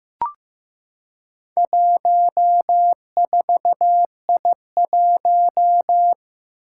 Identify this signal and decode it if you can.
Morse code, heard as a keyed tone, about 15 wpm, 700 Hz. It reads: 14I1
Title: Four Challenging Letter-Number Combinations 15wpm